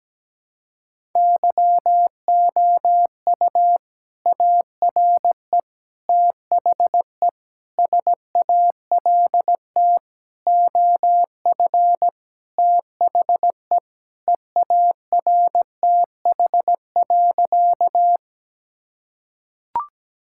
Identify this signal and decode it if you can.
Morse code, heard as a keyed tone, about 17 wpm, 700 Hz. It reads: YOU ARE THE SALT OF THE EARTH.
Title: Sentences from Top 1000 Words - Review Part 5 - 17wpm